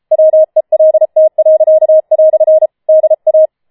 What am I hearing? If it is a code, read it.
Morse code, heard as a keyed tone, about 33 wpm, 630 Hz. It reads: WELT." DA